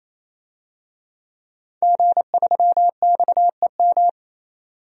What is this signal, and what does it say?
Morse code, heard as a keyed tone, about 28 wpm, 700 Hz. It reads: G3XEM
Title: DXCC-100 Call Signs - 5 Characters - 28wpm